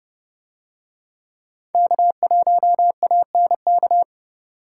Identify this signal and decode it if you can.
Morse code, heard as a keyed tone, about 30 wpm, 700 Hz. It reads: K1ANK